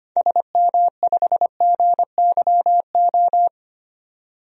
Morse code, heard as a keyed tone, about 25 wpm, 700 Hz. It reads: SM5GYO
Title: DXCC-100 Call Signs - 6 Characters - 25wpm